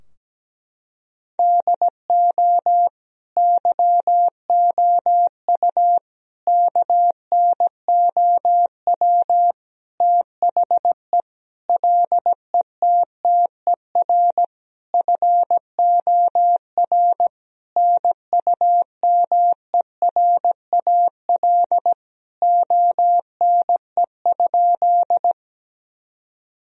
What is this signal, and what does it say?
Morse code, heard as a keyed tone, about 17 wpm, 700 Hz. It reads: DO YOU KNOW THE LETTER FOR NUMERAL ONE?
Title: Sentences from Top 1000 Words - Review Part 2 - 17wpm